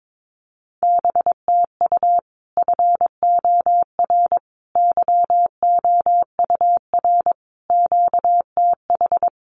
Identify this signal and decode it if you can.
Morse code, heard as a keyed tone, about 22 wpm, 700 Hz. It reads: BTU FOR YOUR QTH
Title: Common QSO Phrases 22wpm